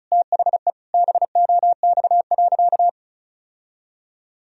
Morse code, heard as a keyed tone, about 34 wpm, 700 Hz. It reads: THE BOX.
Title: Sentences from Top 1000 Words - Review Part 2 - 35wpm